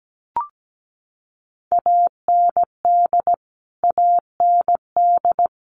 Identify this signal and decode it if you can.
Morse code, heard as a keyed tone, about 17 wpm, 700 Hz. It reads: AND AND
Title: Top 300 Words Repeated 17wpm